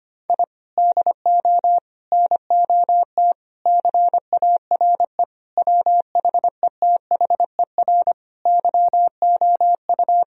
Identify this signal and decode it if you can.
Morse code, heard as a keyed tone, about 25 wpm, 700 Hz. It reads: I DO NOT CARE WHETHER YOU
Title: Sentences from Top 1000 Words - Review Part 3 - 25wpm